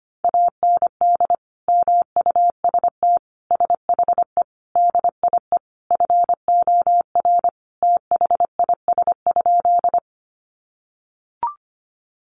Morse code, heard as a keyed tone, about 25 wpm, 700 Hz. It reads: AND MUST SHE DIE FOR THIS?
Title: Sentences from Top 1300 Words 25wpm